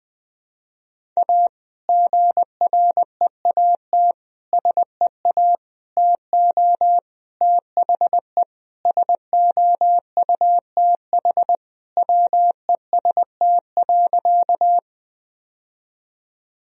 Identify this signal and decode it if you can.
Morse code, heard as a keyed tone, about 20 wpm, 700 Hz. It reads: A GREAT SEA TO THE SOUTH WEST.
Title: Sentences from Top 1000 Words - Review Part 2 - 20wpm